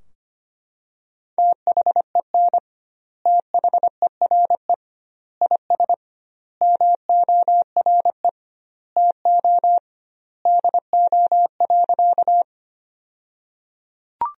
Morse code, heard as a keyed tone, about 25 wpm, 700 Hz. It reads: THEN THERE IS MORE TO DO.
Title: 2x Word-Spacing - Sentences from Top 100 Words - 25wpm